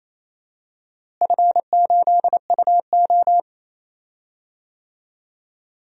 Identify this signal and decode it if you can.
Morse code, heard as a keyed tone, about 28 wpm, 700 Hz. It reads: F8UO
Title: DXCC-100 Call Signs - 4 Characters - Repeated 2x - 28wpm